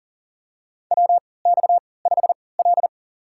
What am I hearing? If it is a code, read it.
Morse code, heard as a keyed tone, about 40 wpm, spaced as for 23 wpm, 700 Hz. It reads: WX5L
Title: USA Call Signs - 4 Characters - 3x Character Spacing - 40wpm